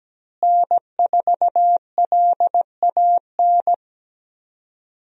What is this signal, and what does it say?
Morse code, heard as a keyed tone, about 17 wpm, 700 Hz. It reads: N4LAN
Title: USA Call Signs - 5 Characters - Repeated 2x - 17wpm